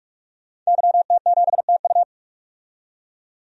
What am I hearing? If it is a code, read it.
Morse code, heard as a keyed tone, about 44 wpm, 700 Hz. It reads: YT7TU
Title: DXCC-100 Call Signs - 5 Characters - 45wpm